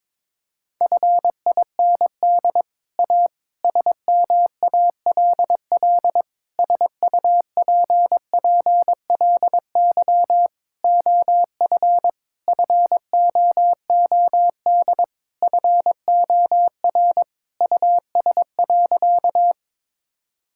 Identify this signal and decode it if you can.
Morse code, heard as a keyed tone, about 22 wpm, 700 Hz. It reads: FIND A SMALL SUPPLY OF FOOD FOR US.
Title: Sentences from Top 1000 Words - Review Part 5 - 22wpm